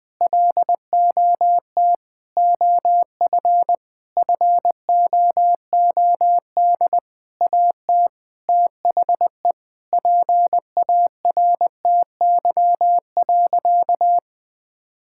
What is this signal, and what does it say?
Morse code, heard as a keyed tone, about 20 wpm, 700 Hz. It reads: LOT OF FOOD AT THE PARTY.
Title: Sentences with 5-Word Phrases - 20wpm